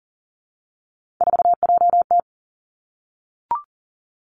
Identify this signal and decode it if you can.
Morse code, heard as a keyed tone, about 39 wpm, 700 Hz. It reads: 4JT
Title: Three Challenging Letter-Number Combinations 40wpm